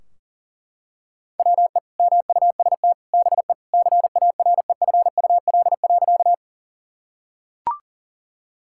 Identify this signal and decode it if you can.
Morse code, heard as a keyed tone, about 40 wpm, 700 Hz. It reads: WE MUST BE CAREFUL.
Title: Sentences from Top 1600 Words 40wpm